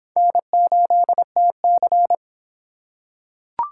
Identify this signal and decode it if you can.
Morse code, heard as a keyed tone, about 24 wpm, 700 Hz. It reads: N8TC